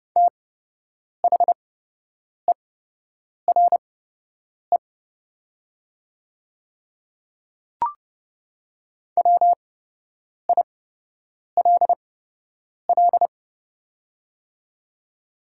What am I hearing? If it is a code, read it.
Morse code, heard as a keyed tone, about 30 wpm, spaced as for 8 wpm, 700 Hz. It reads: THERE WILL
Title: 8x Character Spacing - Top 200 Words - 30wpm